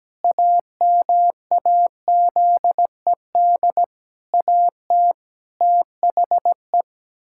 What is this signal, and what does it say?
Morse code, heard as a keyed tone, about 17 wpm, 700 Hz. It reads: AMAZED AT THE
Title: Pangram Challenge v2 - 17wpm